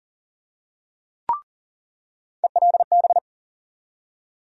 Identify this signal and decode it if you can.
Morse code, heard as a keyed tone, about 40 wpm, 700 Hz. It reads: ELB
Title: USA Call Sign Postfixes - 3 Letters 40wpm